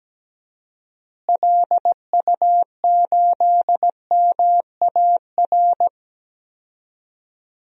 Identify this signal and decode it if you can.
Morse code, heard as a keyed tone, about 17 wpm, 700 Hz. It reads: LU8MAR
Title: DXCC-100 Call Signs - 6 Characters - Repeated 2x - 17wpm